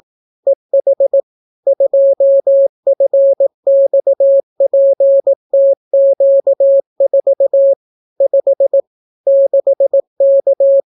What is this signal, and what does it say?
Morse code, heard as a keyed tone, about 18 wpm, 550 Hz. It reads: EH 2FXPTQ4 5 6K